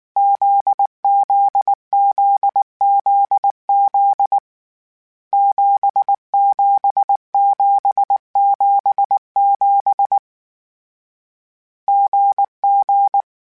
Morse code, heard as a keyed tone, about 19 wpm, 800 Hz. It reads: ZZZZZ 77777 GG